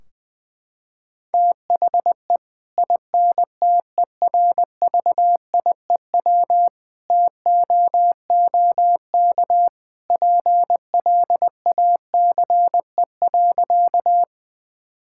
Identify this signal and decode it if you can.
Morse code, heard as a keyed tone, about 20 wpm, 700 Hz. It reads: THE INTERVIEW TOOK PLACE.
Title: Sentences from Top 1600 Words 20wpm